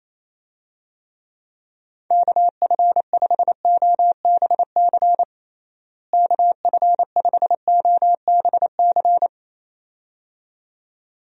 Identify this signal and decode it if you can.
Morse code, heard as a keyed tone, about 28 wpm, 700 Hz. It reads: KF5OBC KF5OBC
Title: USA Call Signs - 6 Characters - Repeated 2x - 28wpm